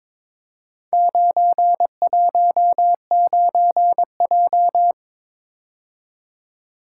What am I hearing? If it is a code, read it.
Morse code, heard as a keyed tone, about 21 wpm, 700 Hz. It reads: 919J
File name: practice / 22wpm / Four Letter-Number Combinations - 22wpm